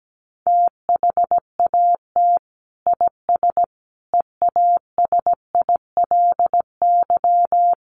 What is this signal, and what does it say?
Morse code, heard as a keyed tone, about 17 wpm, 700 Hz. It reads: THAT IS EASILY
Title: Sentences from Top 900 Words 17wpm